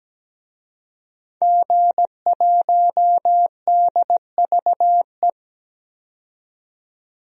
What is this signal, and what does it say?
Morse code, heard as a keyed tone, about 17 wpm, 700 Hz. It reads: G1DVE
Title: DXCC-100 Call Signs - 5 Characters - 17wpm